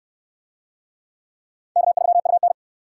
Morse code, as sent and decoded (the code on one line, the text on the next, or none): -.- ...-- ..- -.
K3UN